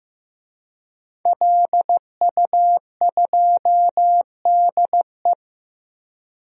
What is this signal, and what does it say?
Morse code, heard as a keyed tone, about 15 wpm, 700 Hz. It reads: LU2DE
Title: DXCC-100 Call Signs - 5 Characters - Repeated 3x - 15wpm